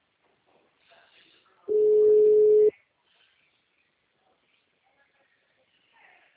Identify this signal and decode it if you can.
Ringback tone (Europe, 425 Hz).
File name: out- -2016- - - .53986